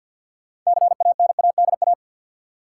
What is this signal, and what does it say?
Morse code, heard as a keyed tone, about 52 wpm, 700 Hz. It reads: CANADA
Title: Sets of 5 Words - New Word Drill - 50wpm